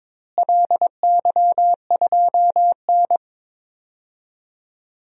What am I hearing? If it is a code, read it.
Morse code, heard as a keyed tone, about 22 wpm, 700 Hz. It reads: LY2N
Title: DXCC-100 Call Signs - 4 Characters - Repeated 2x - 22wpm